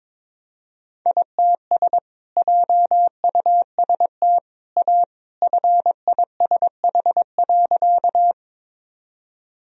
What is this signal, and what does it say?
Morse code, heard as a keyed tone, about 22 wpm, 700 Hz. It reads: ITS JUST A FISH.
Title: Sentences from Top 1000 Words - Review Part 1 - 22wpm